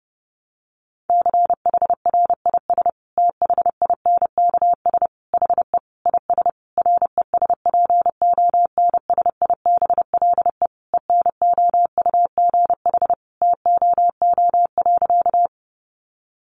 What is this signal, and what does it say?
Morse code, heard as a keyed tone, about 30 wpm, 700 Hz. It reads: CHRIS THINKS HE IS RESPONSIBLE ENOUGH TOO.
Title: Additional Easy Phrases 30wpm